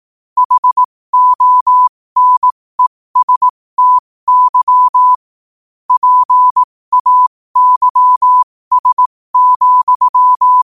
Morse code, heard as a keyed tone, about 18 wpm, spaced as for 15 wpm, 1000 Hz. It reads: HONESTY PAYS,
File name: qotdfiles / 15wpm